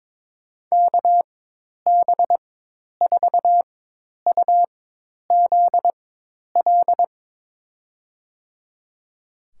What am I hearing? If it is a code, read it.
Morse code, heard as a keyed tone, about 22 wpm, spaced as for 10 wpm, 700 Hz. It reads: KB4UZL
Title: USA Call Signs - 6 Characters - 4x Character Spacing - 22wpm